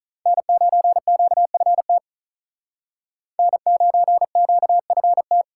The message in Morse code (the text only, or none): N9QFT N9QFT